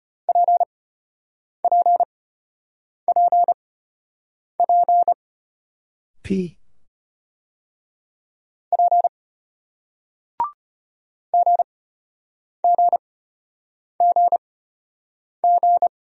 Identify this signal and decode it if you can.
Morse code, heard as a keyed tone, about 33 wpm, spaced as for 8 wpm, 700 Hz. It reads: PPPP P GGGG